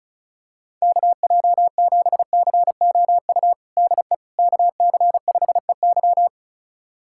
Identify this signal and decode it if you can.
Morse code, heard as a keyed tone, about 34 wpm, 700 Hz. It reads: KJ7COU DE KC5EY
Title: USA Callsign DE Callsign 35wpm